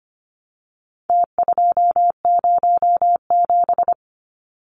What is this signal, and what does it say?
Morse code, heard as a keyed tone, about 25 wpm, 700 Hz. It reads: T207
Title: Numbers - Contest Style - 4-Digit 25wpm